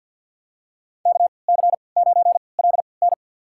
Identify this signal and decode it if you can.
Morse code, heard as a keyed tone, about 52 wpm, spaced as for 29 wpm, 690 Hz. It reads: KC9LN
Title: USA Call Signs - 5 Characters - 3x Character Spacing - 50wpm